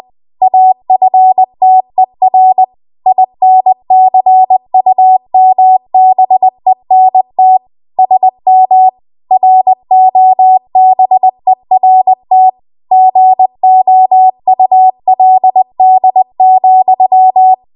Morse code, heard as a keyed tone, about 20 wpm, 750 Hz. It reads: AFTER INCUMBENT SM ROBERT GOULD,
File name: Archive / 20 WPM / 20WPM